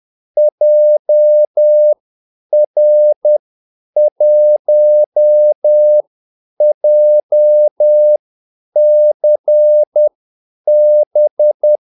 Morse code, heard as a keyed tone, about 10 wpm, spaced as for 8 wpm, 600 Hz. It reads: JR1JCB
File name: Callsigns-10-8wpm